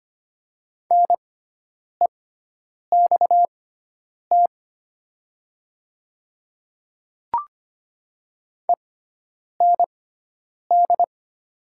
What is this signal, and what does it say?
Morse code, heard as a keyed tone, about 25 wpm, spaced as for 9 wpm, 700 Hz. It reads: NEXT END
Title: 6x Character Spacing - Top 500 Words - 25wpm